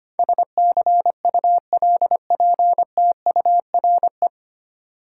Morse code, heard as a keyed tone, about 25 wpm, 690 Hz. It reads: SCULPTURE
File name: practice / 25wpm / 9 Letter Words - First 3 Letters Given - 25wpm